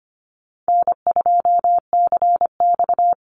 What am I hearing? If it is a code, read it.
Morse code, heard as a keyed tone, about 25 wpm, 700 Hz. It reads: N2CX
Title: Call Signs - 25wpm